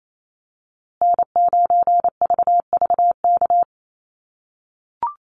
Morse code, heard as a keyed tone, about 28 wpm, 700 Hz. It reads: N9VVK
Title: USA Call Signs Repeated 2x 28wpm